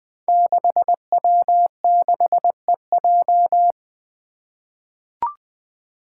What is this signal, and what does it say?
Morse code, heard as a keyed tone, about 20 wpm, 700 Hz. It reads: 6W6EJ
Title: Five Challenging Letter-Number Combinations 20wpm